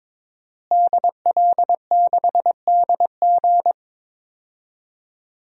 Morse code, heard as a keyed tone, about 22 wpm, 700 Hz. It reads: DL6DG
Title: DXCC-100 Call Signs - 5 Characters - Repeated 3x - 22wpm